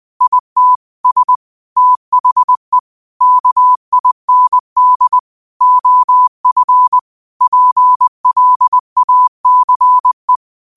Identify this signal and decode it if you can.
Morse code, heard as a keyed tone, about 20 wpm, 1000 Hz. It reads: IT S THE KIND OF PLACE